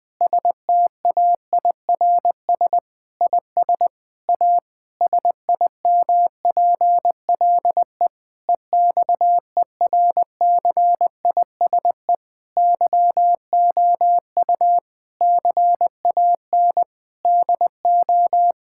Morse code, heard as a keyed tone, about 20 wpm, 700 Hz. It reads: STAIRS IS A SIMPLE EXERCISE YOU CAN DO
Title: Difficult Sentences with Hints - 20wpm